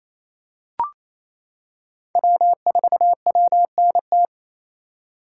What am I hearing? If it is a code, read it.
Morse code, heard as a keyed tone, about 28 wpm, 700 Hz. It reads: W4WNT
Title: K1USN SST QSO Practice - 28wpm